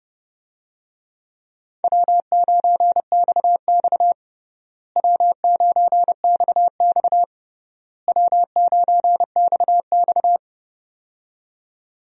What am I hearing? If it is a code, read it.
Morse code, heard as a keyed tone, about 30 wpm, 700 Hz. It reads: W9XX W9XX W9XX
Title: USA Call Signs - 4 Characters - Repeated 3x - 30wpm